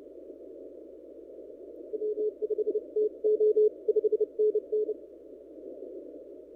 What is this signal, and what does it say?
Morse code, heard as a keyed tone, about 29 wpm, 430 Hz. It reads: W5TO 5NN
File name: n6tr / PT0S